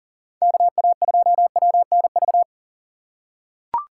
Morse code, heard as a keyed tone, about 40 wpm, 700 Hz. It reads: KA2WNV
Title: USA Call Signs - 2x Character Spacing - 40wpm